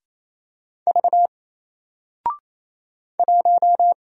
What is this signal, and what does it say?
Morse code, heard as a keyed tone, about 27 wpm, 700 Hz. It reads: V 1
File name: Single-Letter-Number-28wpm